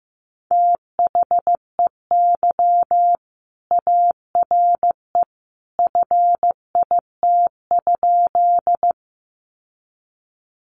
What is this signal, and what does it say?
Morse code, heard as a keyed tone, about 15 wpm, 700 Hz. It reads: THEY ARE FIT?
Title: Sentences from Top 800 Words 15wpm